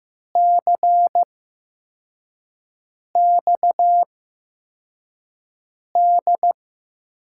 Morse code, heard as a keyed tone, about 15 wpm, spaced as for 4 wpm, 700 Hz. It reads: CXD
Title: USA Call Signs - 6 Characters - 8x Character Spacing - 15wpm